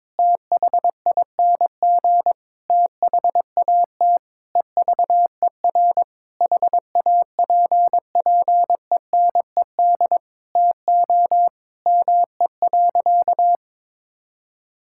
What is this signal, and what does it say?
Morse code, heard as a keyed tone, about 22 wpm, 700 Hz. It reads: THING THAT EVER HAPPENED TO ME.
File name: practice / 22wpm / Sentences with 8-Word Phrases - 22wpm